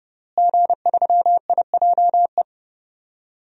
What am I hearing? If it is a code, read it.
Morse code, heard as a keyed tone, about 30 wpm, 700 Hz. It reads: G3IJE
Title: DXCC-100 Call Signs - 5 Characters - Repeated 2x - 30wpm